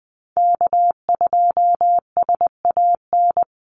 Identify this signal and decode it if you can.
Morse code, heard as a keyed tone, about 20 wpm, 700 Hz. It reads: K2SAN
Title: Call Signs - 20wpm